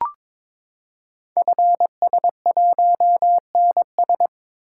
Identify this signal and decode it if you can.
Morse code, heard as a keyed tone, about 22 wpm, 700 Hz. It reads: FS1NS
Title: Code Groups - 5 Letter:Numbers - 22wpm